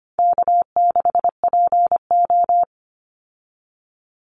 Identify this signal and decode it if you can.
Morse code, heard as a keyed tone, about 25 wpm, 700 Hz. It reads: K6PO
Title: Easy Call Signs - 25wpm